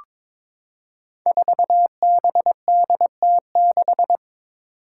Morse code, heard as a keyed tone, about 22 wpm, 700 Hz. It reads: 4BDT6
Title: Five Challenging Letter-Number Combinations 22wpm